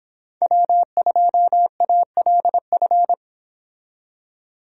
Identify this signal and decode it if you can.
Morse code, heard as a keyed tone, about 25 wpm, 700 Hz. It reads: W2ALF